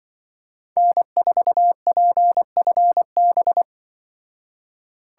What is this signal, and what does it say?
Morse code, heard as a keyed tone, about 24 wpm, 700 Hz. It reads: N4PFB
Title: Speed Racing - USA Call Signs 20wpm